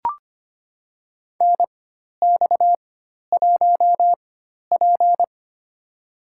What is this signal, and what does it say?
Morse code, heard as a keyed tone, about 25 wpm, spaced as for 12 wpm, 700 Hz. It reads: NX1P
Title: USA Call Signs - 4 Characters - 4x Character Spacing - 25wpm